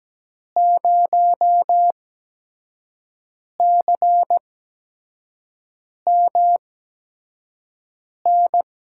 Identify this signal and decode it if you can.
Morse code, heard as a keyed tone, about 17 wpm, spaced as for 5 wpm, 700 Hz. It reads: 0CMN
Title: USA Call Signs - 6 Characters - 8x Character Spacing - 17wpm